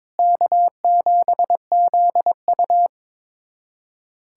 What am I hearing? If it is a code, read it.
Morse code, heard as a keyed tone, about 21 wpm, 700 Hz. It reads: K7ZU
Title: USA Callsign DE Callsign - 4 Characters - 22wpm